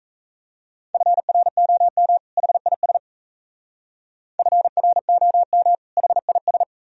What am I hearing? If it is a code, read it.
Morse code, heard as a keyed tone, about 34 wpm, 700 Hz. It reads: FROM HIS FROM HIS
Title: Speed-Racing - Sets of 2 Words 28wpm